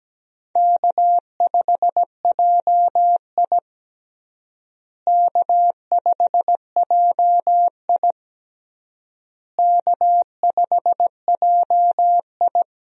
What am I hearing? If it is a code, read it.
Morse code, heard as a keyed tone, about 17 wpm, 700 Hz. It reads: K5JI K5JI K5JI
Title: USA Call Signs - 4 Characters - Repeated 3x - 17wpm